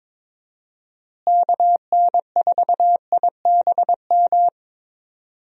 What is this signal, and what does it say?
Morse code, heard as a keyed tone, about 22 wpm, 700 Hz. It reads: KN4IBM
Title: USA Call Signs - 6 Characters - Repeated 2x - 22wpm